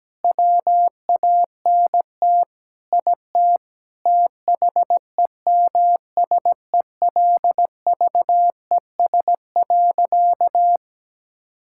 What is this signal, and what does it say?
Morse code, heard as a keyed tone, about 17 wpm, 700 Hz. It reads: WANT IT THEMSELVES.
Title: Sentences from Top 1000 Words - Review Part 5 - 17wpm